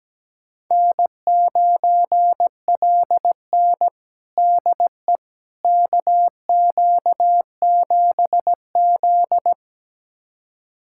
Morse code, heard as a keyed tone, about 17 wpm, 700 Hz. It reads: N9LN DE KQ7Z
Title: USA Callsign DE Callsign - 4 Characters - 17wpm